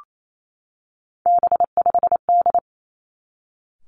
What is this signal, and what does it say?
Morse code, heard as a keyed tone, about 28 wpm, 700 Hz. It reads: B5D
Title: Three Challenging Letter-Number Combinations 28wpm